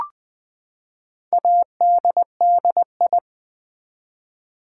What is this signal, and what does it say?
Morse code, heard as a keyed tone, about 20 wpm, 700 Hz. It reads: ADDI
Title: Build A Word - 4 Letters At A Time - 12-Letter Words - 20wpm